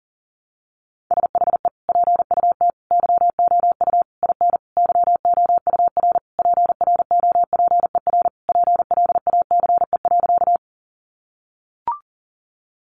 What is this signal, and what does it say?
Morse code, heard as a keyed tone, about 40 wpm, 700 Hz. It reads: SHE PUT YOU IN YOUR PROPER PLACE.
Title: Sentences from Top 800 Words 40wpm